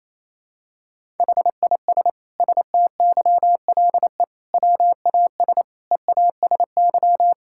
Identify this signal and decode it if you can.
Morse code, heard as a keyed tone, about 28 wpm, 700 Hz. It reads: HIS STYLE WAS EASY
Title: Sentences from Top 1000 Words - Review Part 5 - 28wpm